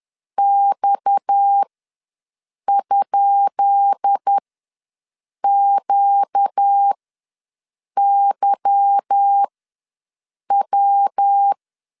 Morse code, heard as a keyed tone, about 10 wpm, spaced as for 6 wpm, 800 Hz. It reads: X?QYW
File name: LEZIONI / 10 SIGNS / 10B INT ws 10wpm 4134